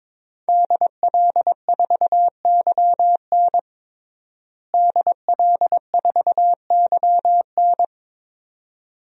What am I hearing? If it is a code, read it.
Morse code, heard as a keyed tone, about 22 wpm, 700 Hz. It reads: DL4YN DL4YN
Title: DXCC-100 Call Signs - 5 Characters - Repeated 2x - 22wpm